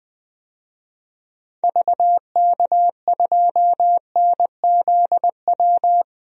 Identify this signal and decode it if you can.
Morse code, heard as a keyed tone, about 20 wpm, 700 Hz. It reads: VK2NZW
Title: DXCC-100 Call Signs - 6 Characters - 20wpm